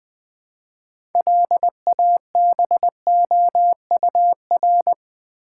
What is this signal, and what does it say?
Morse code, heard as a keyed tone, about 20 wpm, 700 Hz. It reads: LABOUR
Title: Build A Word - 3 Letters At A Time - 6-Letter Words - 20wpm